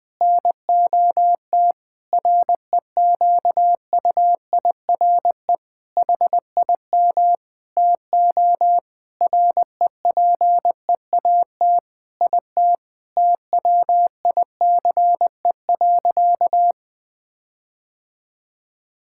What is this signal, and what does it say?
Morse code, heard as a keyed tone, about 20 wpm, 700 Hz. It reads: NOT REQUIRE HIM TO REPEAT IT TWICE.
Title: Sentences from Top 1600 Words 20wpm